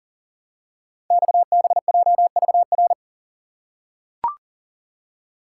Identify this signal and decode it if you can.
Morse code, heard as a keyed tone, about 39 wpm, 690 Hz. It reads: XBJVR
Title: Code Groups - 5 Letters - 40wpm